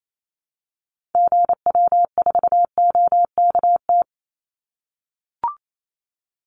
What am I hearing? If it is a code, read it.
Morse code, heard as a keyed tone, about 28 wpm, 700 Hz. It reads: GW4OKT
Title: Call Signs - 28wpm